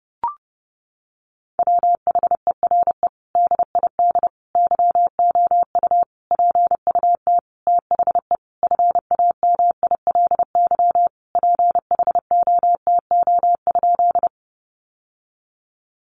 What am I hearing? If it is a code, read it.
Morse code, heard as a keyed tone, about 30 wpm, 700 Hz. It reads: WHERE DID YOU PUT THE FAMILY PHOTO?
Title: Sentences from Top 1200 Words 30wpm